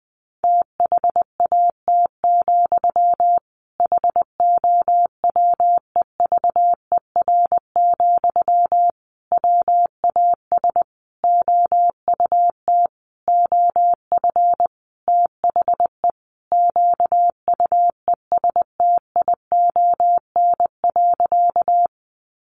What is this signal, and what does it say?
Morse code, heard as a keyed tone, about 20 wpm, 700 Hz. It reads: THAT, HOWEVER, WAS OUT OF THE QUESTION.
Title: Sentences from Top 600 Words 20wpm